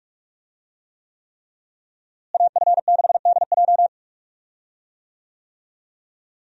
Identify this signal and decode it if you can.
Morse code, heard as a keyed tone, about 41 wpm, 700 Hz. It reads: AF6DJ
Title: USA Call Signs - 5 Characters - Repeated 3x - 45wpm